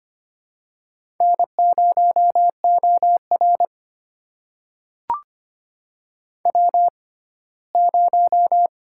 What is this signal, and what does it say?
Morse code, heard as a keyed tone, about 25 wpm, spaced as for 17 wpm, 700 Hz. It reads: N0OR W0
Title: USA Call Signs - 4 Characters - 6x Character Spacing - 25wpm